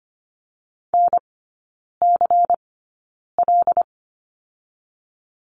Morse code, heard as a keyed tone, about 25 wpm, spaced as for 9 wpm, 700 Hz. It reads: NCL
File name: Lesson 17e - Letter u - Three Characters 25wpm